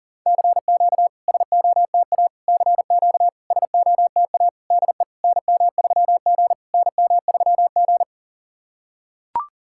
Morse code, heard as a keyed tone, about 39 wpm, 690 Hz. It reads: CQ SOTA CQ SOTA DE NM3G NM3G
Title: SOTA QSO Practice 40wpm